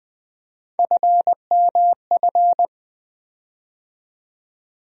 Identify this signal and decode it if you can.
Morse code, heard as a keyed tone, about 20 wpm, 700 Hz. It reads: FMF
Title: USA Call Sign Postfixes - 3 Letters 20wpm